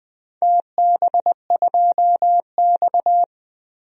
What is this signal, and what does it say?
Morse code, heard as a keyed tone, about 21 wpm, 700 Hz. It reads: TB2X
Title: DXCC-100 Call Signs - 4 Characters - Repeated 2x - 20wpm